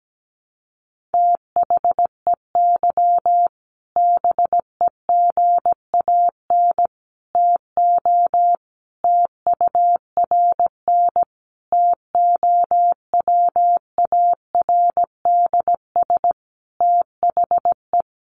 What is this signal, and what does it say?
Morse code, heard as a keyed tone, about 17 wpm, 700 Hz. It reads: THEY BEGAN TO TURN TOWARDS THE
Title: Sentences from Top 1100 Words 17wpm